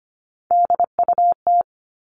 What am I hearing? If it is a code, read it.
Morse code, heard as a keyed tone, about 25 wpm, 700 Hz. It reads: DUT
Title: All Cut Numbers - 3-Digit 25wpm